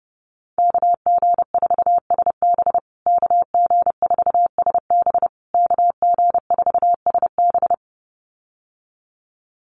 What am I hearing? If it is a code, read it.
Morse code, heard as a keyed tone, about 30 wpm, 700 Hz. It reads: KG4SB KG4SB KG4SB
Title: USA Call Signs Repeated 3x 30wpm